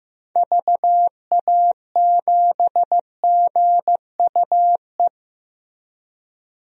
Morse code, heard as a keyed tone, about 15 wpm, 700 Hz. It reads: VA7GUE
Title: DXCC-100 Call Signs - 6 Characters - Repeated 3x - 15wpm